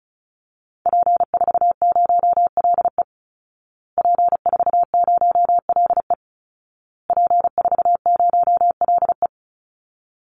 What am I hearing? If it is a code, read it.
Morse code, heard as a keyed tone, about 34 wpm, 700 Hz. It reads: P40LE P40LE P40LE
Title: Call Signs Repeated 2x - 35wpm